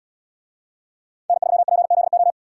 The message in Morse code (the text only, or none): N3PZO